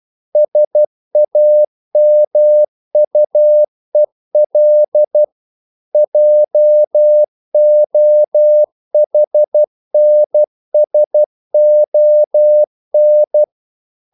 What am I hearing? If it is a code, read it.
Morse code, heard as a keyed tone, about 12 wpm, 600 Hz. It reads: SAMUEL JOHNSON
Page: English quotations in CW (telegraphy - morse)